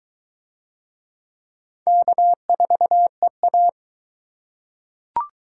Morse code, heard as a keyed tone, about 22 wpm, 700 Hz. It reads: K4EA